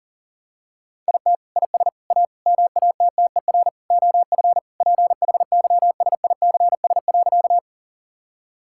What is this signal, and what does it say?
Morse code, heard as a keyed tone, about 40 wpm, 700 Hz. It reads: IT IS A MATTER OF PHYSICS.
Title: Sentences from Top 1000 Words - Review Part 5 - 40wpm